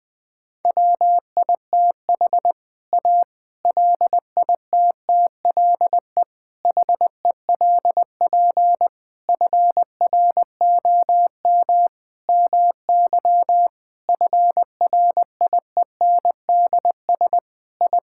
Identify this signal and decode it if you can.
Morse code, heard as a keyed tone, about 20 wpm, 700 Hz. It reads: WITH A LITTLE HELP FROM MY FRIENDS I
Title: Sentences with 7-Word Phrases - 20wpm